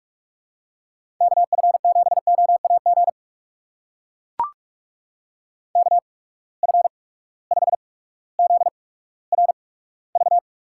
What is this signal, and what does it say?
Morse code, heard as a keyed tone, about 44 wpm, spaced as for 19 wpm, 700 Hz. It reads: KF7OAG KF5ZRV
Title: USA Call Signs - 6 Characters - 8x Character Spacing - 45wpm